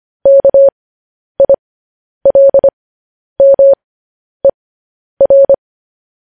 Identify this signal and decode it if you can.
Morse code, heard as a keyed tone, about 25 wpm, spaced as for 10 wpm, 550 Hz. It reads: KILMER
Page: news headlines in morse code at 10 wpm